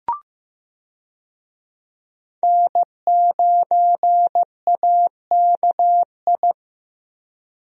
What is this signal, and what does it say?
Morse code, heard as a keyed tone, about 15 wpm, 700 Hz. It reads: N9AKI
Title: USA Call Signs - 5 Characters - Repeated 3x - 15wpm